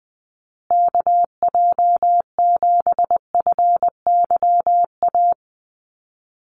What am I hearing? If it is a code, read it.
Morse code, heard as a keyed tone, about 20 wpm, 700 Hz. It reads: KJ7FYA
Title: USA Call Signs Repeated 2x 20wpm